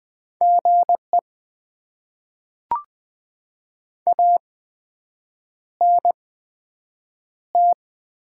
Morse code, heard as a keyed tone, about 20 wpm, spaced as for 7 wpm, 700 Hz. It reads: GE ANT
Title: Top 20 QSO Elements - 8x Character Spacing - 20wpm